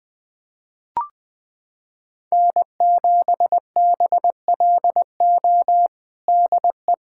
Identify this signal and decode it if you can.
Morse code, heard as a keyed tone, about 20 wpm, 700 Hz. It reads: N7BLO DE
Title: USA Callsign DE Callsign - 5 Characters - 20wpm